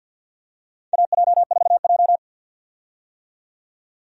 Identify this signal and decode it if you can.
Morse code, heard as a keyed tone, about 47 wpm, 700 Hz. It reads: AJ4J
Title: USA Call Signs - 4 Characters - 50wpm